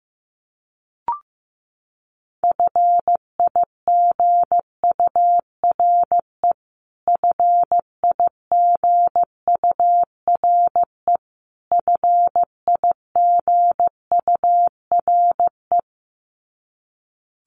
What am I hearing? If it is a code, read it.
Morse code, heard as a keyed tone, about 15 wpm, 700 Hz. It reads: FIGURE FIGURE FIGURE
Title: Top 500 Words Repeated 15wpm